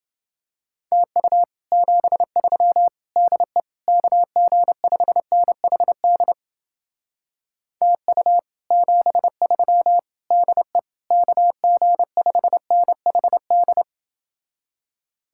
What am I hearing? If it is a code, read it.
Morse code, heard as a keyed tone, about 29 wpm, 700 Hz. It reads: TU 73 DE KG5NHD TU 73 DE KG5NHD